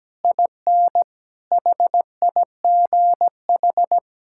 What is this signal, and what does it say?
Morse code, heard as a keyed tone, about 17 wpm, 700 Hz. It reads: IN HIGH
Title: Sentences from Top 1000 Words - Review Part 6 - 17wpm